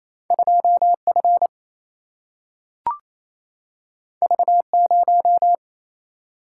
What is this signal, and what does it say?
Morse code, heard as a keyed tone, about 28 wpm, 700 Hz. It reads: 2 F V 0
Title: Single-Letter Single-Number 28wpm